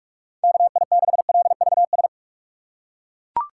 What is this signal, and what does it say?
Morse code, heard as a keyed tone, about 41 wpm, 700 Hz. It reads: KI6LVS
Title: USA Call Signs - 6 Characters - 3x Character Spacing - 45wpm